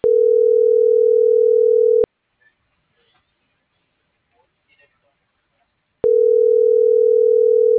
Ringback tone (North America), 440 + 480 Hz.